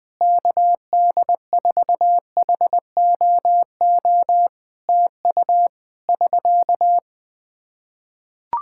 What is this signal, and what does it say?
Morse code, heard as a keyed tone, about 20 wpm, 700 Hz. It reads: KD4HOO TU <SK>
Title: Practice QSO - Exchange 4 - Ending - 20wpm